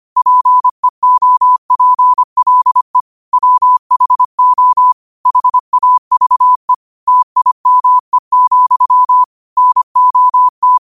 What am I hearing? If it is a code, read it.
Morse code, heard as a keyed tone, about 25 wpm, 1000 Hz. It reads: PEOPLE WHO HAVE TIME, NOT